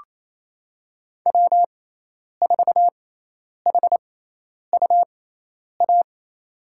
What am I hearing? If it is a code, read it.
Morse code, heard as a keyed tone, about 28 wpm, spaced as for 10 wpm, 700 Hz. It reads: W4HUA
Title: USA Call Signs - 5 Characters - 6x Character Spacing - 28wpm